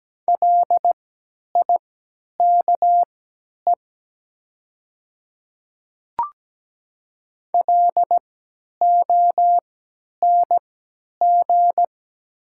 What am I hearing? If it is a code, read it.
Morse code, heard as a keyed tone, about 17 wpm, spaced as for 10 wpm, 700 Hz. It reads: LIKE LONG
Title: 3x Character Spacing - Top 100 Words - 17wpm